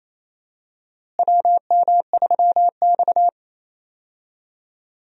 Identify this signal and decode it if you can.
Morse code, heard as a keyed tone, about 27 wpm, 700 Hz. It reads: WM3X
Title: USA Call Signs - 4 Characters - Repeated 3x - 28wpm